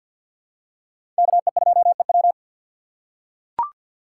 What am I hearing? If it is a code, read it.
Morse code, heard as a keyed tone, about 52 wpm, 700 Hz. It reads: KE2EW
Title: USA Call Signs - 5 Characters - 8x Character Spacing - 50wpm